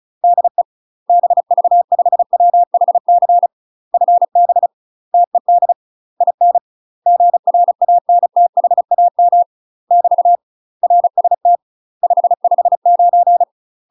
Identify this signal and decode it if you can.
Morse code, heard as a keyed tone, about 36 wpm, 700 Hz. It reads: DE BV5WHC FB TED IN GRANTHAM <BT> RST 559
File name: UK MASTER QSOs TRAINING / 35WPM / UK MASTER QSOs 0038